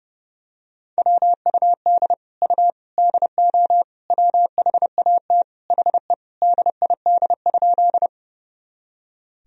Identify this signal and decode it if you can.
Morse code, heard as a keyed tone, about 32 wpm, 700 Hz. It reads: WUD U DO WHAT HE DID?
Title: Sentences from Top 100 Words and Abbreviations 30wpm